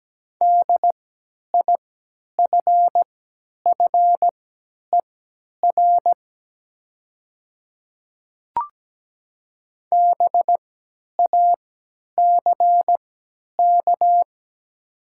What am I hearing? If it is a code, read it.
Morse code, heard as a keyed tone, about 17 wpm, spaced as for 10 wpm, 700 Hz. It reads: DIFFER BACK
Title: 3x Character Spacing - Top 200 Words - 17wpm